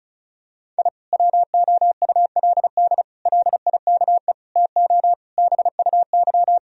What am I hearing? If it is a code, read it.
Morse code, heard as a keyed tone, about 36 wpm, 700 Hz. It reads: I WOULD LIKE TO BUY
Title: Sentences from Top 1000 Words - Review Part 4 - 35wpm